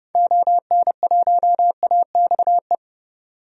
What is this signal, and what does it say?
Morse code, heard as a keyed tone, about 30 wpm, 700 Hz. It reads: ON1AXE
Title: DXCC-100 Call Signs - 6 Characters - 30wpm